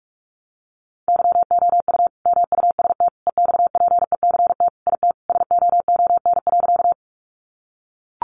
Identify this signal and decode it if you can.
Morse code, heard as a keyed tone, about 44 wpm, 700 Hz. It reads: YOU MUST EXPECT IT SOON.
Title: Sentences from Top 1100 Words 45wpm